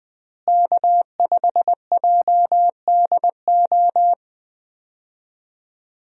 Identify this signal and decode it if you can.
Morse code, heard as a keyed tone, about 20 wpm, 700 Hz. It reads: K5JDO
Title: USA Call Signs - 5 Characters - Repeated 3x - 20wpm